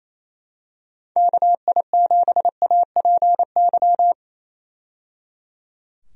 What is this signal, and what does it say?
Morse code, heard as a keyed tone, about 28 wpm, 700 Hz. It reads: KI7APY
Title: USA Call Signs - 6 Characters - 28wpm